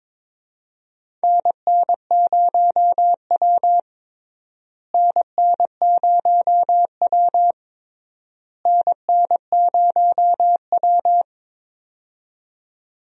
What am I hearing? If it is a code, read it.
Morse code, heard as a keyed tone, about 22 wpm, 700 Hz. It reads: NN0W NN0W NN0W
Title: USA Call Signs - 4 Characters - Repeated 3x - 22wpm